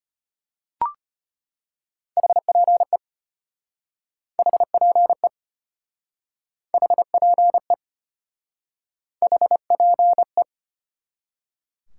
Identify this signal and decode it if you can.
Morse code, heard as a keyed tone, about 34 wpm, 700 Hz. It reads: HPE HPE HPE HPE